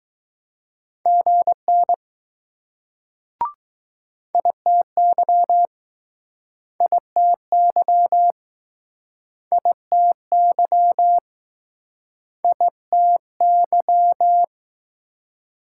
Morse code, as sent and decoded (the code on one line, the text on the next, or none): --. -. / .. - -.-- / .. - -.-- / .. - -.-- / .. - -.--
GN ITY ITY ITY ITY